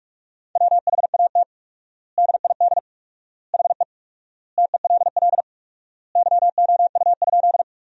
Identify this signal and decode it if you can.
Morse code, heard as a keyed tone, about 43 wpm, 700 Hz. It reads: WHAT DID HE TELL YOU?
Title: 4x Word-Spacing - Sentences from Top 200 Words - 45wpm